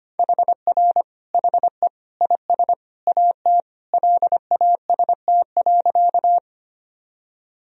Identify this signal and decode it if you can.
Morse code, heard as a keyed tone, about 25 wpm, 700 Hz. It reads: HR HE IS AT LAST.
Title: Sentences from Top 300 Words and Abbreviations 25wpm